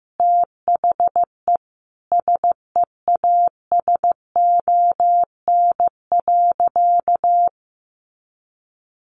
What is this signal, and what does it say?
Morse code, heard as a keyed tone, about 15 wpm, 700 Hz. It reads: THE SEASON.
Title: Sentences from Top 1400 Words 15wpm